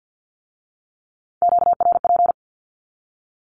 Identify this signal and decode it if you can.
Morse code, heard as a keyed tone, about 50 wpm, 700 Hz. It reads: QRP
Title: Q-Codes with Spoken Meaning - 50wpm